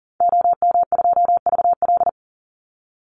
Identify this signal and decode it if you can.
Morse code, heard as a keyed tone, about 40 wpm, 700 Hz. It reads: OM2VL
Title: Call Signs Repeated 2x - 40wpm